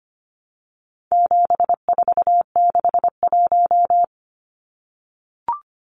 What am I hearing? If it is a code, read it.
Morse code, heard as a keyed tone, about 25 wpm, 700 Hz. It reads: 7461
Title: Numbers - 4-Digit 25wpm